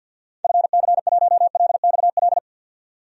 Morse code, heard as a keyed tone, about 51 wpm, 700 Hz. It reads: FC1LXL
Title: DXCC-100 Call Signs - 6 Characters - 50wpm